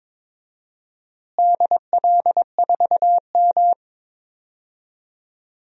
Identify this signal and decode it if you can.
Morse code, heard as a keyed tone, about 22 wpm, 700 Hz. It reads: DL4M
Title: DXCC-100 Call Signs - 4 Characters - Repeated 2x - 22wpm